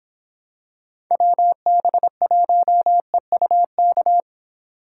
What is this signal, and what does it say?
Morse code, heard as a keyed tone, about 26 wpm, 700 Hz. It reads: WB1EUK